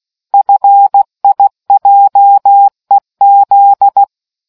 Morse code, heard as a keyed tone, about 16 wpm, 800 Hz. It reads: FIJEZ